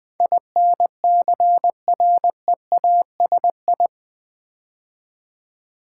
Morse code, heard as a keyed tone, about 20 wpm, 700 Hz. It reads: INCREASI
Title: Build A Word - 4 Letters At A Time - 12-Letter Words - 20wpm